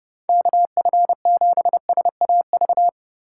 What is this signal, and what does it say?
Morse code, heard as a keyed tone, about 29 wpm, 700 Hz. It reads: KF7SAV